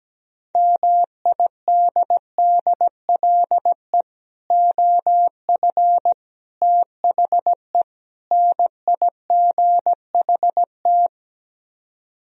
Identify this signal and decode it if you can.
Morse code, heard as a keyed tone, about 17 wpm, 700 Hz. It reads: MIDDLE OF THE NIGHT
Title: Speed-Racing - Sets of 4 Words 17wpm